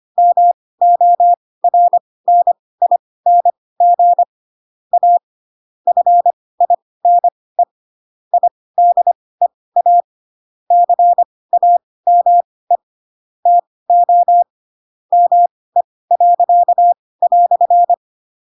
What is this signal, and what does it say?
Morse code, heard as a keyed tone, about 25 wpm, spaced as for 18 wpm, 700 Hz. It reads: MORNING A FINE IDEA CAME TO ME."
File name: Pinocchio / Pinocchio ch2 12 25 ik1hge